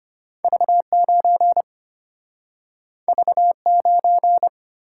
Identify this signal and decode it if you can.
Morse code, heard as a keyed tone, about 34 wpm, 700 Hz. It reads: V9 V9